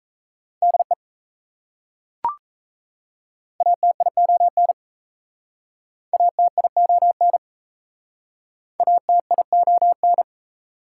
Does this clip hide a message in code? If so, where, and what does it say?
Yes: Morse code from 0.6 to 10.2 s, heard as a keyed tone, about 39 wpm, 690 Hz. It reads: DE ATION ATION ATION